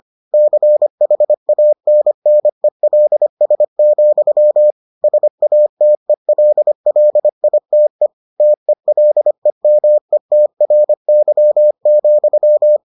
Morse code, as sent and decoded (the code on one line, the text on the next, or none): -.-. .... .- -. -. .-.. ... --..-- / ... .- - . .-.. .-.. .. - . / - . .-.. -- . - .-. -.-- --..--
CHANNELS, SATELLITE TELEMETRY,